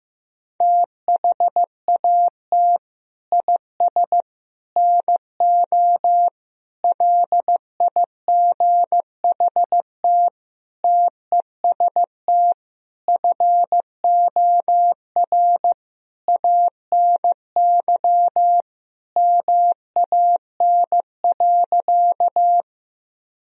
Morse code, heard as a keyed tone, about 15 wpm, 700 Hz. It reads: THAT IS NO LIGHT TEST FOR ANY MAN.
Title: Sentences from Top 1000 Words - Review Part 3 - 15wpm